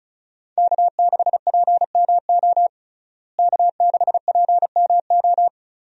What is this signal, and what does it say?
Morse code, heard as a keyed tone, about 36 wpm, 700 Hz. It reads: K6PMO K6PMO
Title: USA Call Signs - 5 Characters - Repeated 2x - 35wpm